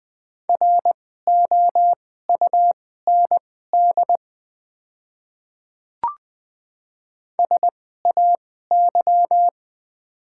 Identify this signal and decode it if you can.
Morse code, heard as a keyed tone, about 20 wpm, spaced as for 14 wpm, 700 Hz. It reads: ROUND SAY
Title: 2x Character Spacing - Top 200 Words - 20wpm